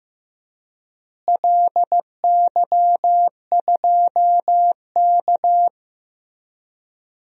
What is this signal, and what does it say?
Morse code, heard as a keyed tone, about 16 wpm, 700 Hz. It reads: LY2K
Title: DXCC-100 Call Signs - 4 Characters - Repeated 3x - 15wpm